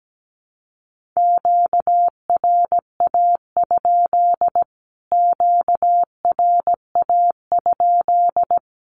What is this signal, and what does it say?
Morse code, heard as a keyed tone, about 17 wpm, 700 Hz. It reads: QRA? QRA?
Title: Q-Codes Repeated 3x 17wpm